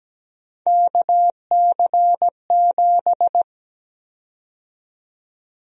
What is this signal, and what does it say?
Morse code, heard as a keyed tone, about 17 wpm, 700 Hz. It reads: KC7
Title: USA Call Sign Prefixes - 17wpm